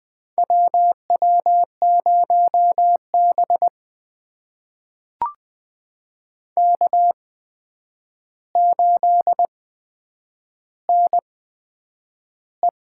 Morse code, heard as a keyed tone, about 20 wpm, spaced as for 9 wpm, 700 Hz. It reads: WW0B K8NE
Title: USA Call Signs - 4 Characters - 8x Character Spacing - 20wpm